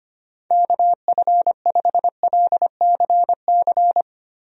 Morse code, heard as a keyed tone, about 25 wpm, 700 Hz. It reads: KF5LCC
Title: USA Call Signs - 6 Characters - Repeated 2x - 25wpm